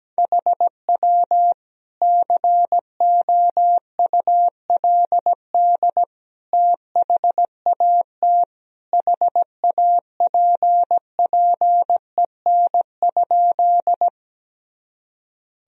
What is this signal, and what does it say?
Morse code, heard as a keyed tone, about 17 wpm, 700 Hz. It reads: HW COULD THAT HAPPEN?
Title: Sentences from Top 400 Words and Abbreviations 17wpm